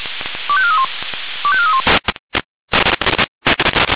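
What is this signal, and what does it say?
Five-tone selective call (ZVEI), heard as quick runs of five tones, tones of about 70 ms. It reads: 25431 25431
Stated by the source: ZVEI five-tone call received by SDR.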